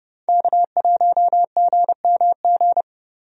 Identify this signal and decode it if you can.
Morse code, heard as a keyed tone, about 32 wpm, 700 Hz. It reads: K1GMG
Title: USA Call Signs - 5 Characters - Repeated 2x - 30wpm